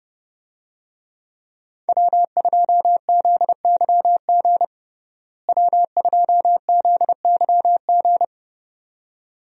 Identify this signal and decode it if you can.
Morse code, heard as a keyed tone, about 30 wpm, 700 Hz. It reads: W2ZYG W2ZYG
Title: USA Call Signs - 5 Characters - Repeated 2x - 30wpm